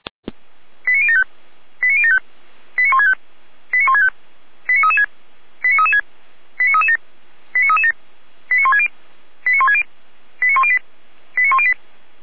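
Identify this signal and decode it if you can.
Five-tone selective call (ZVEI), heard as quick runs of five tones, tones of about 70 ms. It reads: 89075 89075 89165 89165 89336 89336 89337 89337 89160 89160 89108 89108
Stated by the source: ZVEI five-tone selective call 89168, received by SDR.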